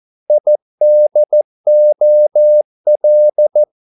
Morse code, heard as a keyed tone, about 14 wpm, 600 Hz. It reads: IDOL
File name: Korte ord DK 14wpm